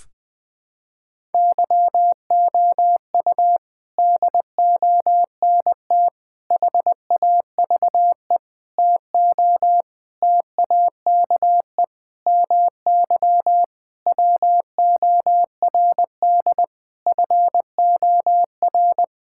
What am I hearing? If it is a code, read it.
Morse code, heard as a keyed tone, about 20 wpm, 700 Hz. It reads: YOU DONT HAVE TO TAKE MY WORD FOR
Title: Sentences with 8-Word Phrases - 20wpm